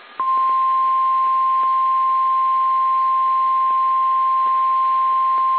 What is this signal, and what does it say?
Emergency Alert System (SAME) tones (the 1050 Hz attention tone). It is heard from 0.7 to 4.8 s.
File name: eas / KSDZKDJL / KSDZKDJL